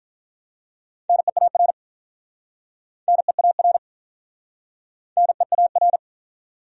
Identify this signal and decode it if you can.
Morse code, heard as a keyed tone, about 47 wpm, 700 Hz. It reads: NEAR NEAR NEAR